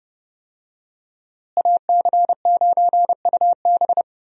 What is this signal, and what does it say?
Morse code, heard as a keyed tone, about 28 wpm, 700 Hz. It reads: AC9UB
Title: USA Call Signs - 5 Characters - Repeated 2x - 30wpm